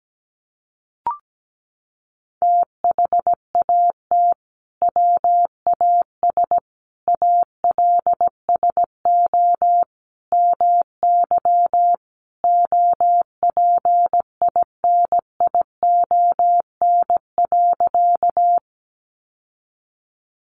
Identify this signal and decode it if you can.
Morse code, heard as a keyed tone, about 17 wpm, 700 Hz. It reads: THAT WAS ALSO MY OPINION.
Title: Sentences from Top 1200 Words 17wpm